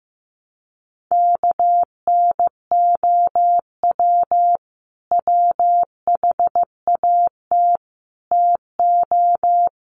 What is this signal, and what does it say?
Morse code, heard as a keyed tone, about 15 wpm, 700 Hz. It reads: KNOW WHAT TO
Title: Sets of 4 Words 15wpm